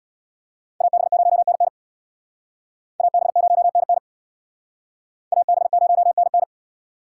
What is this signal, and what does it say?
Morse code, heard as a keyed tone, about 62 wpm, 700 Hz. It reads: AB0NN AB0NN AB0NN
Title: Speed Racing - USA 2x2 Call Signs 50wpm